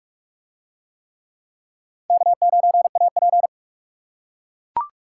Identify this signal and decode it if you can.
Morse code, heard as a keyed tone, about 37 wpm, 700 Hz. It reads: K9AP
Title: Speed Racing - USA Call Signs 30wpm